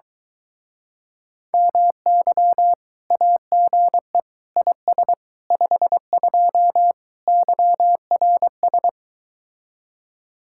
Morse code, heard as a keyed tone, about 23 wpm, 700 Hz. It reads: MY AGE IS 52 YRS